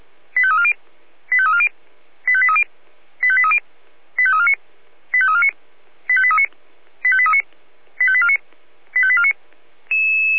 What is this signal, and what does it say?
Five-tone selective call (ZVEI), heard as quick runs of five tones, tones of about 70 ms. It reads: 85340 85340 85830 85830 85349 85349 85839 85839 85849 85849